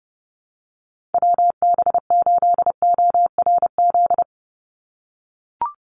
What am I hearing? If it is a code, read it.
Morse code, heard as a keyed tone, about 30 wpm, 700 Hz. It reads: WB8ORZ
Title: USA Call Signs 30wpm